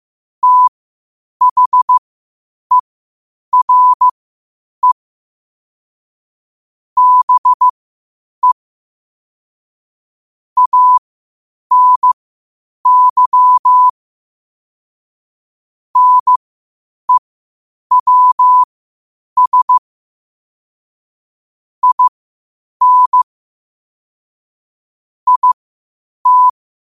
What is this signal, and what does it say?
Morse code, heard as a keyed tone, about 15 wpm, spaced as for 8 wpm, 1000 Hz. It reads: THERE BE ANY NEWS IN IT
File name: qotdfiles / 8wpm